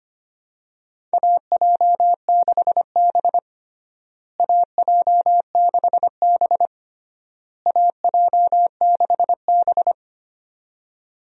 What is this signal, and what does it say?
Morse code, heard as a keyed tone, about 25 wpm, 700 Hz. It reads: AJ6B AJ6B AJ6B
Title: USA Call Signs - 4 Characters - Repeated 3x - 25wpm